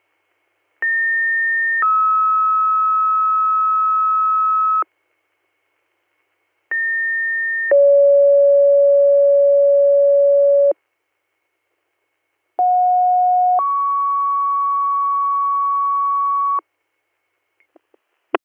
Two-tone page, heard as a one-second tone, then a three-second tone, about 1800 then 1290 Hz, about 1800 then 560 Hz, about 730 then 1120 Hz. It is heard from 0.8 to 16.6 s.